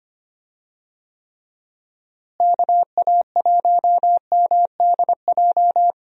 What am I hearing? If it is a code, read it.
Morse code, heard as a keyed tone, about 25 wpm, 700 Hz. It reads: KA1MDJ
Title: USA Call Signs - 6 Characters - Repeated 3x - 25wpm